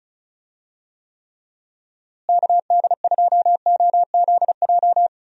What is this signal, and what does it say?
Morse code, heard as a keyed tone, about 34 wpm, 700 Hz. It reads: KD2OZJ
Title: USA Call Signs - 6 Characters - Repeated 2x - 35wpm